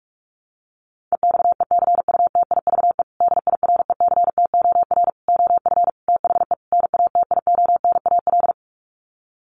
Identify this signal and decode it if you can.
Morse code, heard as a keyed tone, about 44 wpm, 700 Hz. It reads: EXECUTIVE DIRECTOR OF THE NATIONAL
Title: Sets of 5 Words Encore2 45wpm